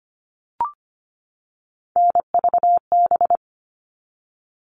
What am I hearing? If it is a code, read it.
Morse code, heard as a keyed tone, about 25 wpm, 700 Hz. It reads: NVB
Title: All Cut Numbers - 3-Digit 25wpm